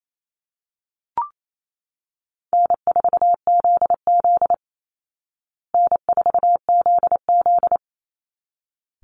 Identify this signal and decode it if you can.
Morse code, heard as a keyed tone, about 28 wpm, 700 Hz. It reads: N4ZZ N4ZZ
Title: Easy Call Signs Repeated 2x - 28wpm